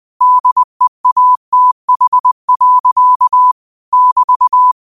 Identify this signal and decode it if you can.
Morse code, heard as a keyed tone, about 20 wpm, 1000 Hz. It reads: DEATH. <BT>
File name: qotdfiles / 20wpm